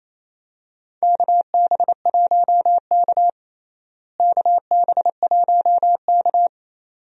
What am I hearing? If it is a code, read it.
Morse code, heard as a keyed tone, about 28 wpm, 700 Hz. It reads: KB1K KB1K
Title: USA Call Signs - 4 Characters - Repeated 2x - 28wpm